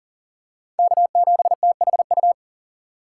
Morse code, heard as a keyed tone, about 40 wpm, 700 Hz. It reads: K7THU
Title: USA Call Signs - 5 Characters - 4x Character Spacing - 40wpm